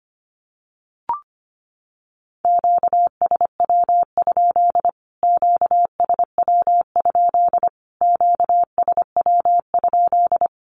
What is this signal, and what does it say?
Morse code, heard as a keyed tone, about 25 wpm, 700 Hz. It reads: QSW? QSW? QSW?
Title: Q-Codes Repeated 3x 25wpm